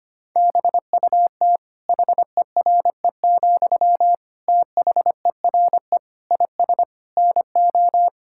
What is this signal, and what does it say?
Morse code, heard as a keyed tone, about 25 wpm, 700 Hz. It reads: BUT HERE, THERE IS NO
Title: Sentences from Top 1000 Words - Review Part 6 - 25wpm